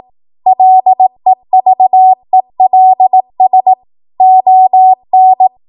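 Morse code, heard as a keyed tone, about 18 wpm, 750 Hz. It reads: LEVELS ON
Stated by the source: 18 WPM Morse code practice from a national amateur radio association's code archive.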